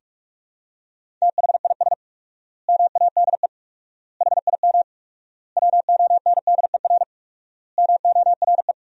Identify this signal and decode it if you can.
Morse code, heard as a keyed tone, about 46 wpm, 700 Hz. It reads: THIS MADE HIM WONDER MORE
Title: 4x Word-Spacing - Sentences from Top 500 Words - 45wpm